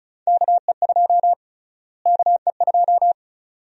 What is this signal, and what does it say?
Morse code, heard as a keyed tone, about 36 wpm, 700 Hz. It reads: KE2 KE2
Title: USA Call Sign Prefixes - Repeated 2x - 35wpm